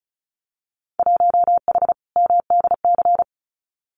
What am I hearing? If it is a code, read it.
Morse code, heard as a keyed tone, about 36 wpm, 700 Hz. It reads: 1H MDC
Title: Winter Field Day Exchanges 35wpm